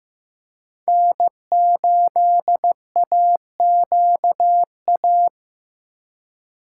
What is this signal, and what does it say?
Morse code, heard as a keyed tone, about 18 wpm, 700 Hz. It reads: N8AQA
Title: Speed Racing - USA 1x3 Call Signs 15wpm